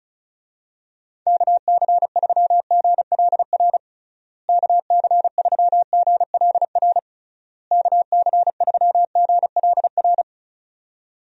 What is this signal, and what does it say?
Morse code, heard as a keyed tone, about 36 wpm, 700 Hz. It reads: KC3GLR KC3GLR KC3GLR
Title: USA Call Signs - 6 Characters - Repeated 3x - 35wpm